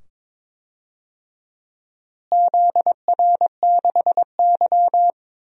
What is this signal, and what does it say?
Morse code, heard as a keyed tone, about 21 wpm, 700 Hz. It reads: ZR6Y
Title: DXCC-100 Call Signs - 4 Characters - Repeated 2x - 22wpm